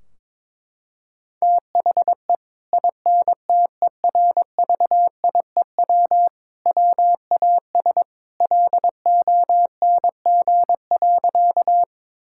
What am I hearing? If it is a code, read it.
Morse code, heard as a keyed tone, about 22 wpm, 700 Hz. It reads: THE INTERVIEW WAS LONG.
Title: Sentences from Top 1600 Words 22wpm